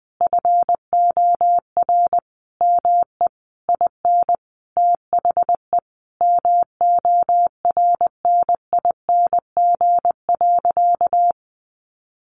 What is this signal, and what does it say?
Morse code, heard as a keyed tone, about 20 wpm, 700 Hz. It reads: FOR ME IN THE MORNING.
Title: Sentences from Top 1500 Words 20wpm